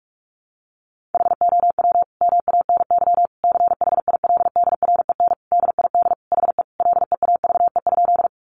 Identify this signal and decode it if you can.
Morse code, heard as a keyed tone, about 44 wpm, 700 Hz. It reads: HOW MANY CHILDREN DID HE LEAVE?
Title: Sentences from Top 400 Words 45wpm